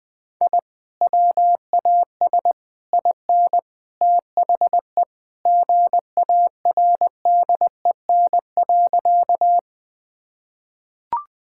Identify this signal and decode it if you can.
Morse code, heard as a keyed tone, about 20 wpm, 700 Hz. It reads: I WAS IN THE GARDEN.
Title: Sentences with 3-Word Phrases - 20wpm